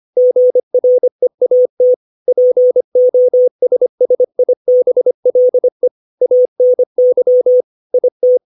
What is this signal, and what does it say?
Morse code, heard as a keyed tone, about 25 wpm, 500 Hz. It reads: GREAT POSSIBLE ANY IT
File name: 632 words / 632 Words 25WPM